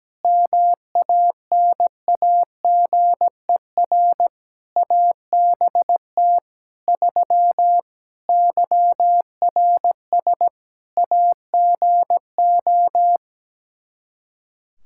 Morse code, heard as a keyed tone, about 17 wpm, 700 Hz. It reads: MANAGER ABT 3 YRS AGO
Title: Practice QSO - Exchange 3 - More Info - 17wpm